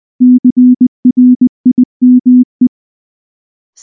Morse code, heard as a keyed tone, about 19 wpm, 250 Hz. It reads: CRIME
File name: morse / morse-training-20WPM-3